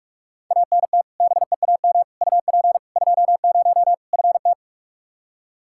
Morse code, heard as a keyed tone, about 46 wpm, 700 Hz. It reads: ANT BEAM UP 20 FT
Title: K1USN SST QSO Practice 45wpm